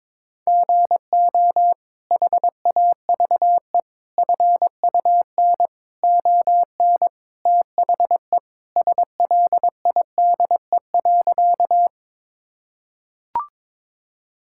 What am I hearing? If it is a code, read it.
Morse code, heard as a keyed tone, about 22 wpm, 700 Hz. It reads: GO HAVE FUN ON THE SLIDE.
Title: Sentences from Top 1600 Words 22wpm